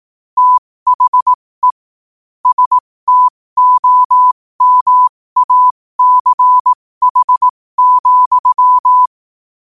Morse code, heard as a keyed tone, about 18 wpm, spaced as for 15 wpm, 1000 Hz. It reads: THE STOMACH,
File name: qotdfiles / 15wpm